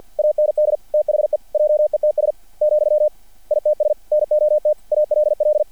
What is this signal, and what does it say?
Morse code, heard as a keyed tone, about 50 wpm, 600 Hz. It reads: AND THE JETS , ITS NOT ALL
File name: archived / CRSnet